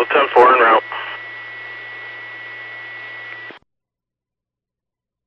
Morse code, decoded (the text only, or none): none